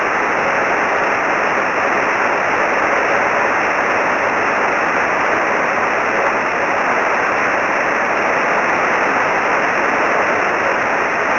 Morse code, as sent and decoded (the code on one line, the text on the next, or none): none